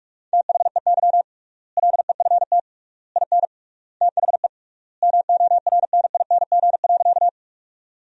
Morse code, heard as a keyed tone, about 46 wpm, 700 Hz. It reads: THEY LEFT IN THE MORNING.
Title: 3x Word-Spacing - Sentences from Top 400 Words - 45wpm